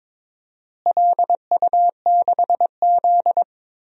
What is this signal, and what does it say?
Morse code, heard as a keyed tone, about 22 wpm, 700 Hz. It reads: LU6Z
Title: DXCC-100 Call Signs - 4 Characters - Repeated 2x - 22wpm